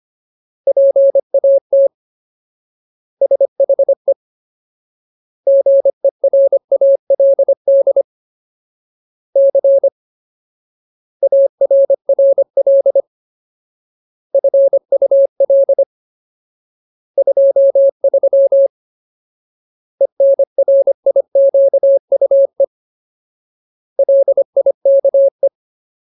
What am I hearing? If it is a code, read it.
Morse code, heard as a keyed tone, about 25 wpm, 550 Hz. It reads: PAT SHE GERALD C ARRL FUL 23 ENRIQUE LIKE
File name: Ultimate Word List / 25WPM Ultimate